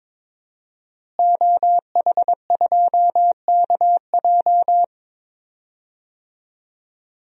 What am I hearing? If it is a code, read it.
Morse code, heard as a keyed tone, about 22 wpm, 700 Hz. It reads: OH2KJ
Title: DXCC-100 Call Signs - 5 Characters - Repeated 2x - 22wpm